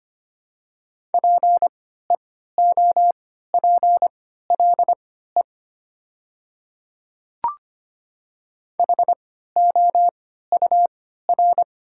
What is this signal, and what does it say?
Morse code, heard as a keyed tone, about 25 wpm, spaced as for 14 wpm, 700 Hz. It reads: PEOPLE HOUR
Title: 3x Character Spacing - Top 500 Words - 25wpm